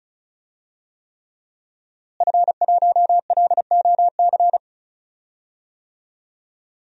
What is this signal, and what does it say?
Morse code, heard as a keyed tone, about 36 wpm, 700 Hz. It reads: F1LOC
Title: DXCC-100 Call Signs - 5 Characters - Repeated 3x - 35wpm